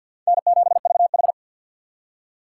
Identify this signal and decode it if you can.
Morse code, heard as a keyed tone, about 53 wpm, 700 Hz. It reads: N7VH
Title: USA Call Signs - 4 Characters - 6x Character Spacing - 50wpm